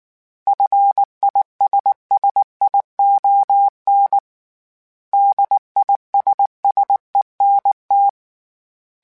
Morse code, heard as a keyed tone, about 19 wpm, 800 Hz. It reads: FISSION DISSENT